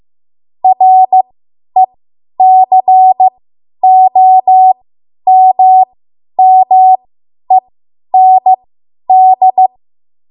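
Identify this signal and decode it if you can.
Morse code, heard as a keyed tone, about 15 wpm, spaced as for 10 wpm, 750 Hz. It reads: RECOMMEND
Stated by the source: Morse code practice run at 10 WPM from a code practice archive.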